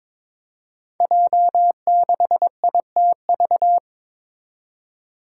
Morse code, heard as a keyed tone, about 22 wpm, 700 Hz. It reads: J6ITV
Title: Five Challenging Letter-Number Combinations 22wpm